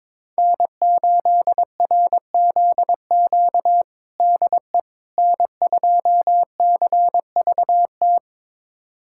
Morse code, heard as a keyed tone, about 22 wpm, 700 Hz. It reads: N8RZQ DE N2CVT
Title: USA Callsign DE Callsign - 5 Characters - 22wpm